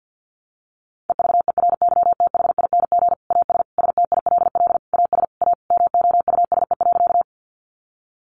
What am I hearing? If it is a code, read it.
Morse code, heard as a keyed tone, about 50 wpm, 700 Hz. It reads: EVERYTHING AS STILL AS A MOUSE.
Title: Sentences from Top 700 Words 50wpm